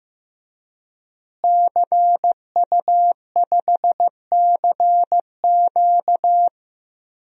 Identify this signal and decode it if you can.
Morse code, heard as a keyed tone, about 15 wpm, 700 Hz. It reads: CU5CQ
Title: DXCC-100 Call Signs - 5 Characters - Repeated 3x - 15wpm